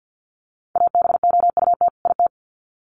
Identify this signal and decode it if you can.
Morse code, heard as a keyed tone, about 50 wpm, 700 Hz. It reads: ABOUT IT
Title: Sets of 2 Words 50wpm